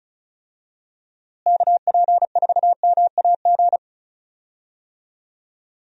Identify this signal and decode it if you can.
Morse code, heard as a keyed tone, about 36 wpm, 700 Hz. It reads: KP4MAG
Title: USA Call Signs - 6 Characters - Repeated 3x - 35wpm